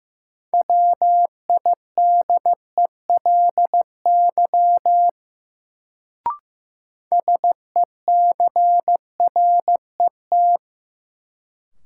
Morse code, heard as a keyed tone, about 15 wpm, 700 Hz. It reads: WIDELY SECRET
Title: Sentences from Top 1200 Words - New Word Drill - 15wpm